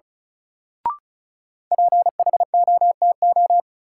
Morse code, heard as a keyed tone, about 34 wpm, 700 Hz. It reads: PHOTO
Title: Sentences from Top 1200 Words - New Word Drill - 35wpm